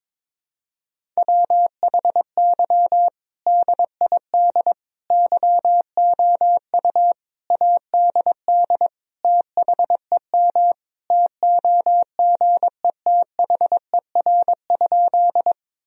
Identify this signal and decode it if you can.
Morse code, heard as a keyed tone, about 22 wpm, 700 Hz. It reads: WHY DID YOU ADD THEM TOGETHER?
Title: Sentences from Top 1000 Words - Review Part 1 - 22wpm